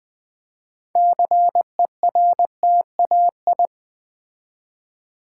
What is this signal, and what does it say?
Morse code, heard as a keyed tone, about 20 wpm, 700 Hz. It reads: CERTAI
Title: Build A Word - 3 Letters At A Time - 9-Letter Words - 20wpm